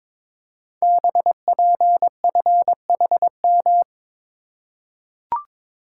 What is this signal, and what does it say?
Morse code, heard as a keyed tone, about 22 wpm, 700 Hz. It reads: BPFHM
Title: Code Groups - 5 Letters - 22wpm